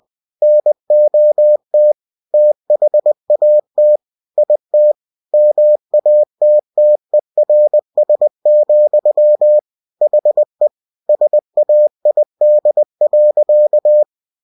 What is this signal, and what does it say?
Morse code, heard as a keyed tone, about 20 wpm, 600 Hz. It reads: NOT THAT IT MATTERS, HE SAID.
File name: cw-books / winnie-the-pooh / winnie-20wpm-page-0053